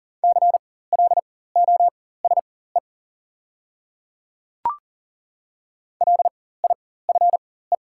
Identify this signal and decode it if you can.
Morse code, heard as a keyed tone, about 40 wpm, spaced as for 19 wpm, 690 Hz. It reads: CLOSE LIFE
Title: 4x Character Spacing - Top 300 Words - 40wpm